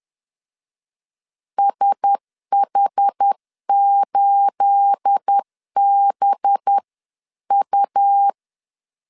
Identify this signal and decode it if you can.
Morse code, heard as a keyed tone, about 10 wpm, 800 Hz. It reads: SH8BU
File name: LEZIONI / 9 28 10wpm 2842